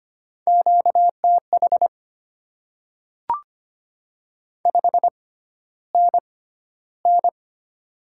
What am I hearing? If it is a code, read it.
Morse code, heard as a keyed tone, about 25 wpm, spaced as for 13 wpm, 690 Hz. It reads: QTH 5NN
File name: Top 20 QSO Elements - 6x Character Spacing - 25wpm